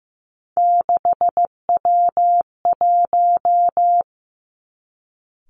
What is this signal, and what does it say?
Morse code, heard as a keyed tone, about 16 wpm, 700 Hz. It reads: 6W1
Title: Three Challenging Letter-Number Combinations 15wpm